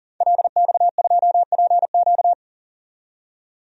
Morse code, heard as a keyed tone, about 40 wpm, 700 Hz. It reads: LX2PQ
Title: DXCC-100 Call Signs - 5 Characters - 40wpm